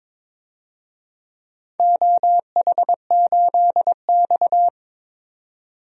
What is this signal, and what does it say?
Morse code, heard as a keyed tone, about 22 wpm, 700 Hz. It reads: OH8X
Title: DXCC-100 Call Signs - 4 Characters - 22wpm